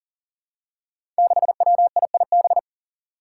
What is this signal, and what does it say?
Morse code, heard as a keyed tone, about 40 wpm, 700 Hz. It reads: 6WIIB
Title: Five Challenging Letter-Number Combinations 40wpm